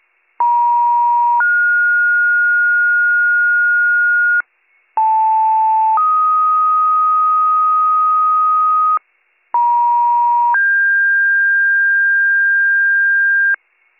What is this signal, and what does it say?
Two-tone page, heard as a one-second tone, then a three-second tone, about 950 then 1430 Hz, about 880 then 1250 Hz, about 950 then 1600 Hz, with noise. It is heard from 0.4 to 13.5 s.